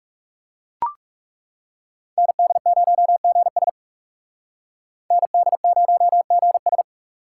Morse code, heard as a keyed tone, about 38 wpm, 700 Hz. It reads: ND0GS ND0GS